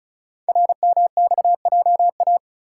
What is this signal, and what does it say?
Morse code, heard as a keyed tone, about 36 wpm, 700 Hz. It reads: RMXJA
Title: Code Groups - 5 Letters - 35wpm